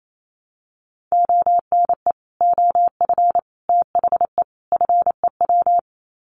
Morse code, heard as a keyed tone, about 28 wpm, 700 Hz. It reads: ONE OF THE FEW
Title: Sets of 4 Words 28wpm